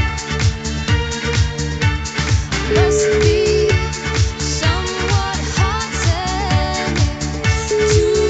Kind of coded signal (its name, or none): telephone call progress tone